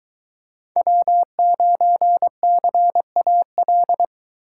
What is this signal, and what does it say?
Morse code, heard as a keyed tone, about 22 wpm, 700 Hz. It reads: W9CAL